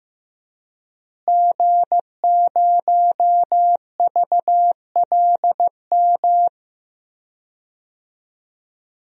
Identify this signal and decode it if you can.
Morse code, heard as a keyed tone, about 15 wpm, 700 Hz. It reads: G0VLM
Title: DXCC-100 Call Signs - 5 Characters - Repeated 2x - 15wpm